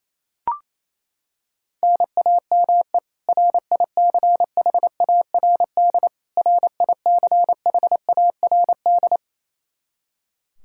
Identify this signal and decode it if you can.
Morse code, heard as a keyed tone, about 28 wpm, 700 Hz. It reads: NAME RICHARD RICHARD
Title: Practice QSO - Exchange 1 - Beginning - 28wpm